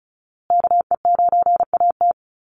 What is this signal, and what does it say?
Morse code, heard as a keyed tone, about 34 wpm, 700 Hz. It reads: KE9AT
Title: USA Call Signs Repeated 2x 35wpm